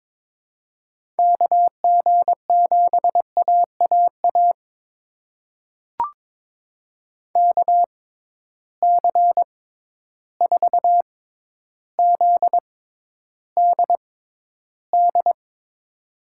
Morse code, heard as a keyed tone, about 22 wpm, spaced as for 11 wpm, 700 Hz. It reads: KG7AAA KC4ZDD
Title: USA Call Signs - 6 Characters - 6x Character Spacing - 22wpm